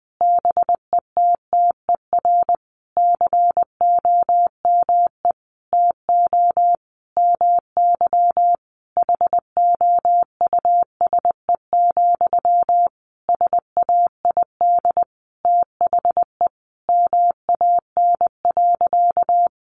Morse code, heard as a keyed tone, about 20 wpm, 700 Hz. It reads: BETTER COME TO MY HOUSE, SAID THE MAN.
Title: Sentences from Top 400 Words 20wpm